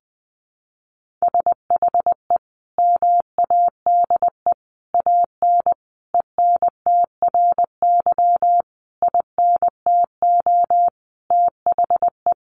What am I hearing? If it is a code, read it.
Morse code, heard as a keyed tone, about 20 wpm, 700 Hz. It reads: SHE MADE AN ENTRY INTO THE
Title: Sentences from Top 1500 Words 20wpm